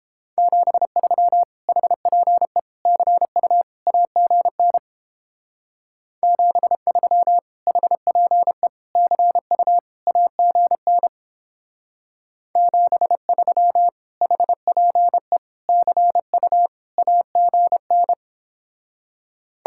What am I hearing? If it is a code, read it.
Morse code, heard as a keyed tone, about 31 wpm, 700 Hz. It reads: 73 HPE CU AGN 73 HPE CU AGN 73 HPE CU AGN